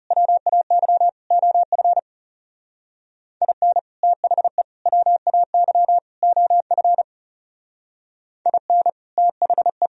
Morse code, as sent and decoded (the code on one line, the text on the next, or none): .-- .- -.-- / --- ..-. / .. -. / - .... / .-- .- -.-- / --- ..-. / .. -. / - .... .
WAY OF IN THE WAY OF IN THE